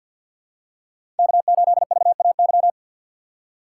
Morse code, heard as a keyed tone, about 51 wpm, 700 Hz. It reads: K8VAY
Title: USA Call Signs - 5 Characters - 6x Character Spacing - 50wpm